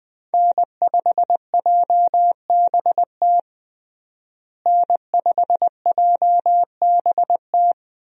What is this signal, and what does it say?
Morse code, heard as a keyed tone, about 21 wpm, 700 Hz. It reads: N5JBT N5JBT
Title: USA Call Signs - 5 Characters - Repeated 3x - 20wpm